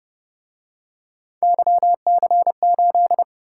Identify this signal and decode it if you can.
Morse code, heard as a keyed tone, about 30 wpm, 700 Hz. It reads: YC8